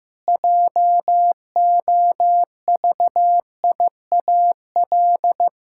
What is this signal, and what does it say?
Morse code, heard as a keyed tone, about 15 wpm, 700 Hz. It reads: JOVIAL
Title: Pangram Challenge v2 - 15wpm